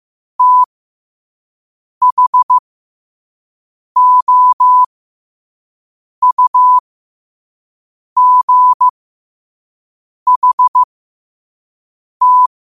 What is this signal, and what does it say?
Morse code, heard as a keyed tone, about 15 wpm, spaced as for 5 wpm, 1000 Hz. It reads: THOUGHT
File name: qotdfiles / 5wpm